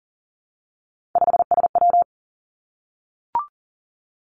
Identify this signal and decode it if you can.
Morse code, heard as a keyed tone, about 41 wpm, 700 Hz. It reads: 5SW
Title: Three Challenging Letter-Number Combinations 40wpm